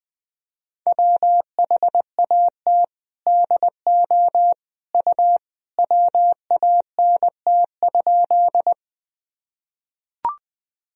Morse code, heard as a keyed tone, about 20 wpm, 700 Hz. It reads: WHAT DO U WANT?
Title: Sentences from Top 200 Words and Abbreviations 20wpm